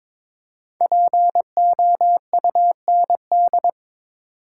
Morse code, heard as a keyed tone, about 22 wpm, 700 Hz. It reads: POUND
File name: 5 Letter Words - First 3 Letters Given - 22wpm